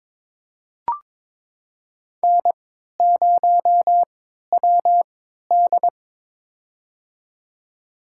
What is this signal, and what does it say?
Morse code, heard as a keyed tone, about 21 wpm, spaced as for 12 wpm, 700 Hz. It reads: N0WD
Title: USA Call Signs - 4 Characters - 3x Character Spacing - 22wpm